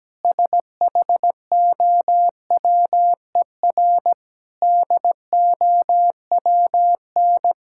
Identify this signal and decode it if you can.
Morse code, heard as a keyed tone, about 17 wpm, 700 Hz. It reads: SHOWER DOWN
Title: Sentences from Top 1600 Words 17wpm